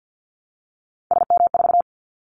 Morse code, heard as a keyed tone, about 50 wpm, 700 Hz. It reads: SM4
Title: Callsign Prefixes for Top 25 DXCC - 50wpm